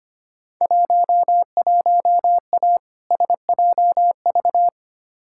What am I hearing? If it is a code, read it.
Morse code, heard as a keyed tone, about 25 wpm, 700 Hz. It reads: 11A SJV
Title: Field Day Exchanges Encore - 25wpm